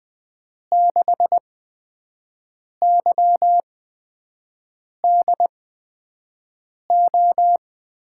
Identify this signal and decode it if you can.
Morse code, heard as a keyed tone, about 20 wpm, spaced as for 5 wpm, 700 Hz. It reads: 6YDO
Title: USA Call Signs - 6 Characters - 8x Character Spacing - 20wpm